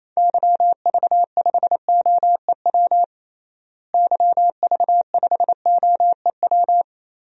Morse code, heard as a keyed tone, about 28 wpm, 700 Hz. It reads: YV5OEW YV5OEW
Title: DXCC-100 Call Signs - 6 Characters - Repeated 2x - 28wpm